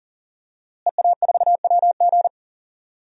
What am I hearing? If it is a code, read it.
Morse code, heard as a keyed tone, about 40 wpm, 700 Hz. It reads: EA4WG
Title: DXCC-100 Call Signs - 5 Characters - Repeated 3x - 40wpm